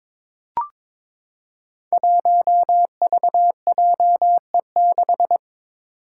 Morse code, heard as a keyed tone, about 22 wpm, 700 Hz. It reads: 1VJE6
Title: Five Challenging Letter-Number Combinations 22wpm